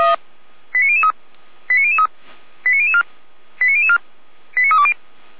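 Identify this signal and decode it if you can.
Five-tone selective call (ZVEI), heard as quick runs of five tones, tones of about 70 ms. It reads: 89003 89003 89004 89004 89320
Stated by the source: ZVEI five-tone call 89001 received by SDR, archived by fire brigade Feuerwehr Orlamünde.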